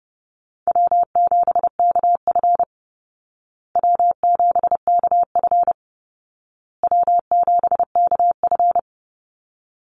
Morse code, heard as a keyed tone, about 29 wpm, 700 Hz. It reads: W7KF W7KF W7KF
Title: Easy Call Signs Repeated 3x - 30wpm